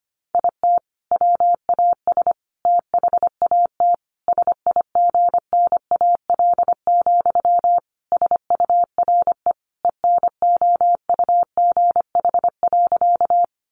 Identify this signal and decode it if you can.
Morse code, heard as a keyed tone, about 25 wpm, 700 Hz. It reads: IT WAS THAT SIGNAL, SURE ENOUGH.
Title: Sentences from Top 1400 Words 25wpm